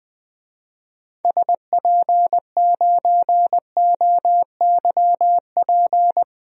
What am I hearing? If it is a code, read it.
Morse code, heard as a keyed tone, about 20 wpm, 700 Hz. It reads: SP9OYP
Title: DXCC-100 Call Signs - 6 Characters - 20wpm